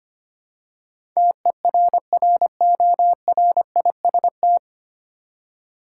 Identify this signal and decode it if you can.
Morse code, heard as a keyed tone, about 25 wpm, 700 Hz. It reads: TERRORIST
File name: practice / 25wpm / 9 Letter Words - First 3 Letters Given - 25wpm